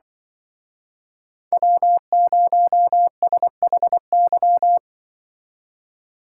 Morse code, heard as a keyed tone, about 25 wpm, 700 Hz. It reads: W0SHY
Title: Speed Racing - USA 1x3 Call Signs 20wpm